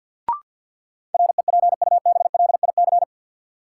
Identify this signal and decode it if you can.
Morse code, heard as a keyed tone, about 53 wpm, 700 Hz. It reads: REPUBLIC
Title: Sentences from Top 1500 Words - New Word Drill - 50wpm